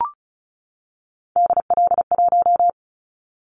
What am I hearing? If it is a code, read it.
Morse code, heard as a keyed tone, about 36 wpm, 700 Hz. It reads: DL1
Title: Callsign Prefixes for Top 25 DXCC - 35wpm